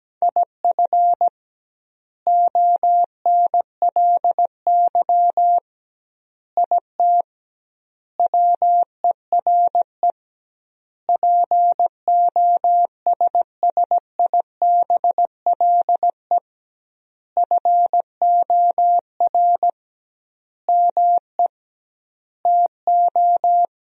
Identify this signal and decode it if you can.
Morse code, heard as a keyed tone, about 17 wpm, 700 Hz. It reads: IF ONLY IT WERE POSSIBLE FOR ME TO
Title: 2x Word-Spacing - Sentences from Top 500 Words - 17wpm